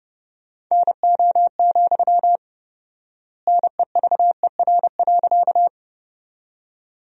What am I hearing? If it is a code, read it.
Morse code, heard as a keyed tone, about 32 wpm, 700 Hz. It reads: NO, NEVER.
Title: 4x Word-Spacing - Sentences from Top 300 Words - 30wpm